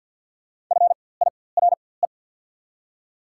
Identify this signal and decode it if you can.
Morse code, heard as a keyed tone, about 54 wpm, spaced as for 24 wpm, 690 Hz. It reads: FIRE
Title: 4x Character Spacing - Top 500 Words - 50wpm